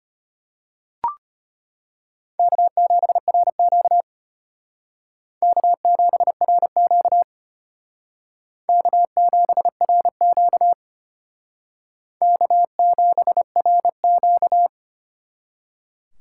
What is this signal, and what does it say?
Morse code, heard as a keyed tone, about 33 wpm, 700 Hz. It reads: K7RQ K7RQ K7RQ K7RQ